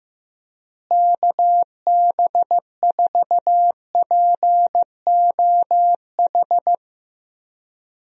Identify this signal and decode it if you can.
Morse code, heard as a keyed tone, about 15 wpm, 700 Hz. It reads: KB4POH
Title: USA Call Signs - 6 Characters - Repeated 3x - 15wpm